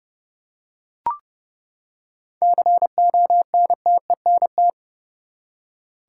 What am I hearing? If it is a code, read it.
Morse code, heard as a keyed tone, about 32 wpm, 700 Hz. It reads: CONTENT
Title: Seven Letter Words - 30wpm